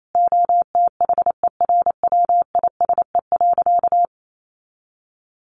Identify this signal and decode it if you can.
Morse code, heard as a keyed tone, about 28 wpm, 700 Hz. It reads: OTHERWISE.
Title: Sentences from Top 1200 Words 28wpm